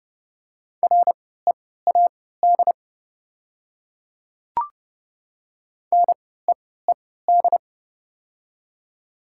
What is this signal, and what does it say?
Morse code, heard as a keyed tone, about 31 wpm, spaced as for 17 wpm, 690 Hz. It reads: READ NEED
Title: 3x Character Spacing - Top 300 Words - 30wpm